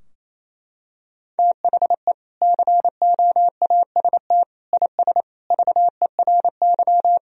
Morse code, heard as a keyed tone, about 28 wpm, 700 Hz. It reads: THE COAST IS VERY
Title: Sentences from Top 1000 Words - Review Part 4 - 28wpm